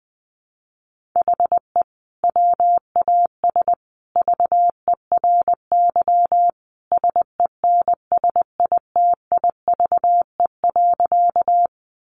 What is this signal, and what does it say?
Morse code, heard as a keyed tone, about 20 wpm, 700 Hz. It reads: HE WAS VERY SENSITIVE.
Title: Sentences from Top 1400 Words 20wpm